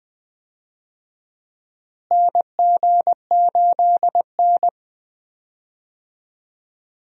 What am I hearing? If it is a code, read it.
Morse code, heard as a keyed tone, about 20 wpm, 700 Hz. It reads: NG8N
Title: USA Call Signs - 4 Characters - Repeated 2x - 20wpm